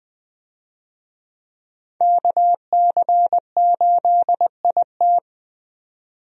Morse code, heard as a keyed tone, about 20 wpm, 700 Hz. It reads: KC8IT
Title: USA Call Signs - 5 Characters - 20wpm